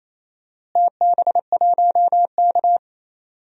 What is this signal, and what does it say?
Morse code, heard as a keyed tone, about 28 wpm, 700 Hz. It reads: TB1K
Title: DXCC-100 Call Signs - 4 Characters - Repeated 2x - 28wpm